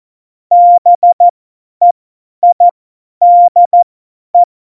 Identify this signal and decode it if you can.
Morse code, heard as a keyed tone, about 14 wpm, spaced as for 10 wpm, 700 Hz. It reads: BEIDE
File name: Morsekurs / Witze / Witz 24.Dezember 2019, 50 F70